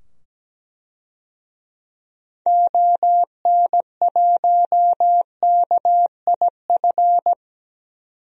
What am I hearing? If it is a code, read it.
Morse code, heard as a keyed tone, about 17 wpm, 700 Hz. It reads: ON1KIF
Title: DXCC-100 Call Signs - 6 Characters - 17wpm